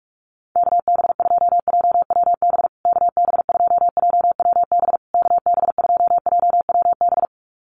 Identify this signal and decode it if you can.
Morse code, heard as a keyed tone, about 44 wpm, 700 Hz. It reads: KB2JWB KB2JWB KB2JWB
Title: USA Call Signs Repeated 3x 45wpm